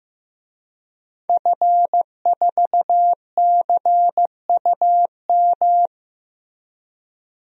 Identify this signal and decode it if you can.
Morse code, heard as a keyed tone, about 15 wpm, 700 Hz. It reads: F4CUM
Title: DXCC-100 Call Signs - 5 Characters - Repeated 2x - 15wpm